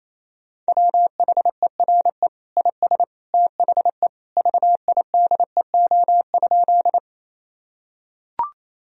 Morse code, heard as a keyed tone, about 28 wpm, 700 Hz. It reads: WHERE IS THE VIDEO?
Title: Sentences from Top 1000 Words - Review Part 4 - 28wpm